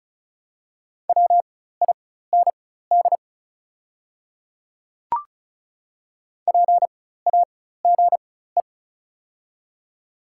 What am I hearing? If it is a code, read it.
Morse code, heard as a keyed tone, about 34 wpm, spaced as for 16 wpm, 700 Hz. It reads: WIND PAGE
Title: 4x Character Spacing - Top 500 Words - 35wpm